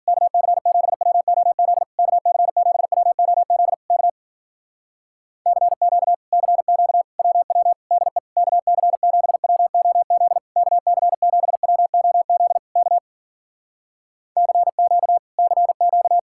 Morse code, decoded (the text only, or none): KC7WOZ KC7WOZ K CQ CQ WW DE KC7WOZ KC7WOZ K CQ CQ